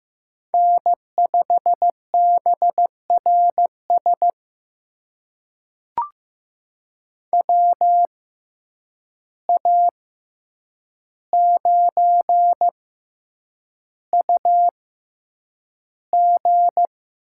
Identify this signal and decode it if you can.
Morse code, heard as a keyed tone, about 15 wpm, spaced as for 8 wpm, 700 Hz. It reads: N5BRS WA9UG
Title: USA Call Signs - 6x Character Spacing - 15wpm